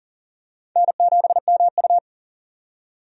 Morse code, heard as a keyed tone, about 40 wpm, 700 Hz. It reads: N7MU
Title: USA Call Signs - 4 Characters - 40wpm